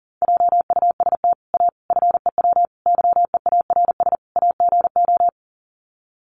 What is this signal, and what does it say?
Morse code, heard as a keyed tone, about 40 wpm, 700 Hz. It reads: JUST A FEW YEARS AGO
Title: Sets of 5 Words Encore 40wpm